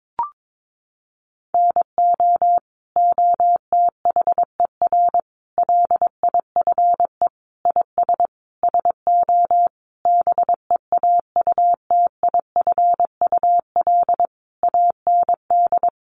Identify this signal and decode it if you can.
Morse code, heard as a keyed tone, about 22 wpm, 700 Hz. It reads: NO OTHER LIFE IS SO BEAUTIFUL AND
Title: Sentences from Top 1500 Words 22wpm